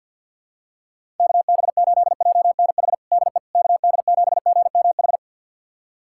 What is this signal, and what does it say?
Morse code, heard as a keyed tone, about 52 wpm, 700 Hz. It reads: KB8JNH DE KD7GMH
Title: USA Callsign DE Callsign - 6 Characters - 50wpm